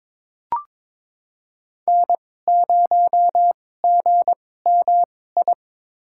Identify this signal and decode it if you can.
Morse code, heard as a keyed tone, about 22 wpm, spaced as for 16 wpm, 700 Hz. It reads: N0GMI
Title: USA Call Signs - 5 Characters - 2x Character Spacing - 22wpm